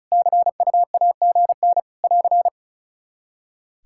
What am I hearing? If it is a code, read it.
Morse code, heard as a keyed tone, about 36 wpm, 700 Hz. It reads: CUAGN <AR>
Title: K1USN SST QSO Practice 35wpm